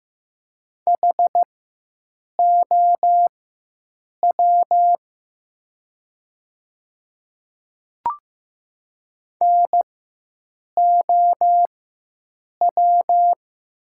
Morse code, heard as a keyed tone, about 15 wpm, spaced as for 7 wpm, 700 Hz. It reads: HOW NOW
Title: 4x Character Spacing - Top 100 Words - 15wpm